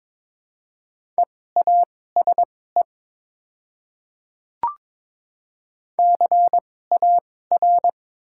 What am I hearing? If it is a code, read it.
Morse code, heard as a keyed tone, about 22 wpm, spaced as for 16 wpm, 700 Hz. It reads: EASE CAR
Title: 2x Character Spacing - Top 300 Words - 22wpm